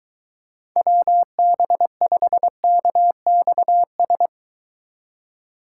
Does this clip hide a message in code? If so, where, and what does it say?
Yes: Morse code from 0.8 to 4.3 s, heard as a keyed tone, about 22 wpm, 700 Hz. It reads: WB5KXS